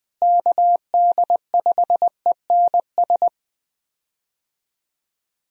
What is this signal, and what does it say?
Morse code, heard as a keyed tone, about 20 wpm, 700 Hz. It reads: KD5ENS
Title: USA Call Signs - 6 Characters - Repeated 2x - 20wpm